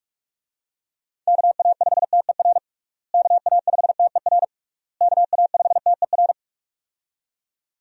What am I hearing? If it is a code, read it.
Morse code, heard as a keyed tone, about 46 wpm, 700 Hz. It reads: KA5TER KA5TER KA5TER
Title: USA Call Signs - 6 Characters - Repeated 3x - 45wpm